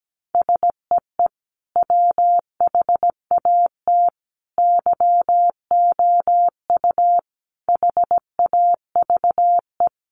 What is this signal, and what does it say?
Morse code, heard as a keyed tone, about 17 wpm, 700 Hz. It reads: SEE WHAT YOU HAVE
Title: Sentences from Top 500 Words 17wpm